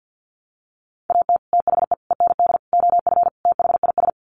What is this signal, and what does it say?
Morse code, heard as a keyed tone, about 48 wpm, 700 Hz. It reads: AT THE END OF THIS
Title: Sets of 5 Words 50wpm